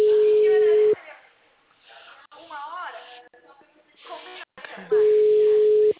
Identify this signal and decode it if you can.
Ringback tone (Europe, 425 Hz), with speech.